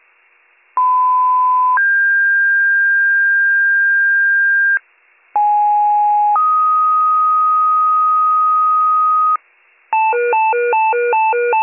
Two-tone page, heard as a one-second tone, then a three-second tone, about 1010 then 1600 Hz, about 830 then 1250 Hz, with noise. It is heard from 0.8 to 9.4 s.